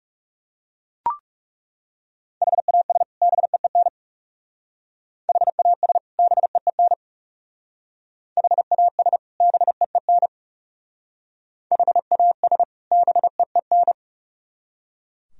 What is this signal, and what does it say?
Morse code, heard as a keyed tone, about 37 wpm, 700 Hz. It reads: HAS BEEN HAS BEEN HAS BEEN HAS BEEN
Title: Speed-Racing - Sets of 2 Words 30wpm